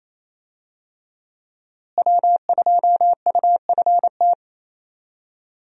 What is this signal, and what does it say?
Morse code, heard as a keyed tone, about 28 wpm, 700 Hz. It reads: W2UFT
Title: USA Call Signs - 5 Characters - Repeated 3x - 28wpm